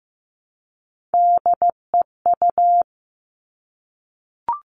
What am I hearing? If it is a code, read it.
Morse code, heard as a keyed tone, about 15 wpm, 700 Hz. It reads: DEU
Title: All Cut Numbers - 3-Digit 15wpm